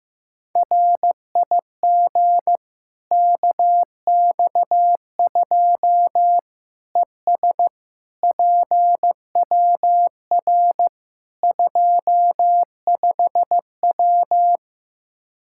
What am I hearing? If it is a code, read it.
Morse code, heard as a keyed tone, about 15 wpm, 700 Hz. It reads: RIG KX2 ES PWR 25W
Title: Practice QSO - Exchange 2 - Info - 15wpm